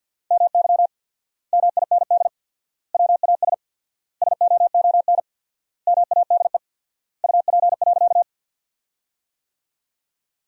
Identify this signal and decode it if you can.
Morse code, heard as a keyed tone, about 51 wpm, 700 Hz. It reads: MY MIND WAS SOON MADE UP.
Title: 4x Word-Spacing - Sentences from Top 500 Words - 50wpm